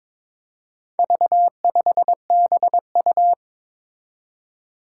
Morse code, heard as a keyed tone, about 22 wpm, 700 Hz. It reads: V5BU
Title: Four Challenging Letter-Number Combinations 22wpm